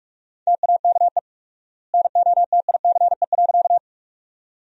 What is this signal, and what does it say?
Morse code, heard as a keyed tone, about 46 wpm, 700 Hz. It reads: TAKE NOTICE.
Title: 4x Word-Spacing - Sentences from Top 400 Words - 45wpm